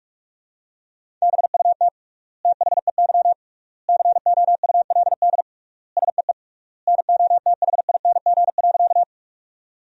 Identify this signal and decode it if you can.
Morse code, heard as a keyed tone, about 46 wpm, 700 Hz. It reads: BUT THEY COULD SEE NOTHING.
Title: 3x Word-Spacing - Sentences from Top 500 Words - 45wpm